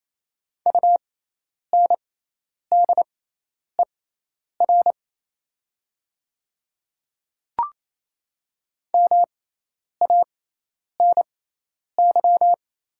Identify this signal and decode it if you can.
Morse code, heard as a keyed tone, about 28 wpm, spaced as for 10 wpm, 700 Hz. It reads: UNDER MANY
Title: 6x Character Spacing - Top 200 Words - 28wpm